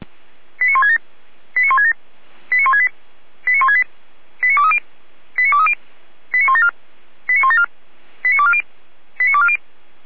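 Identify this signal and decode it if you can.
Five-tone selective call (ZVEI), heard as quick runs of five tones, tones of about 70 ms. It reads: 89167 89167 89168 89168 89230 89230 89174 89174 89240 89240
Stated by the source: ZVEI five-tone selective call 89115 received on SDR.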